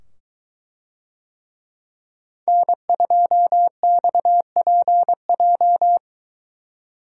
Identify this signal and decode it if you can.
Morse code, heard as a keyed tone, about 23 wpm, 700 Hz. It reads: N2XPJ